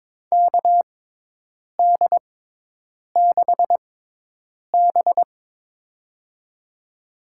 Morse code, heard as a keyed tone, about 22 wpm, spaced as for 8 wpm, 700 Hz. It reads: KD6B
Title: USA Call Signs - 4 Characters - 6x Character Spacing - 22wpm